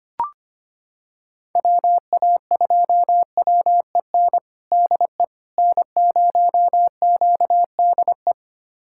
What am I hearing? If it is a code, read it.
Morse code, heard as a keyed tone, about 25 wpm, 700 Hz. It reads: WA2WEN DE N0QDE
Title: USA Callsign DE Callsign 25wpm